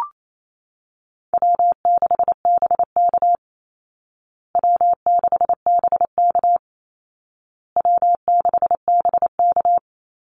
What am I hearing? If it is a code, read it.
Morse code, heard as a keyed tone, about 28 wpm, 700 Hz. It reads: W6BK W6BK W6BK
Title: Easy Call Signs Repeated 3x - 28wpm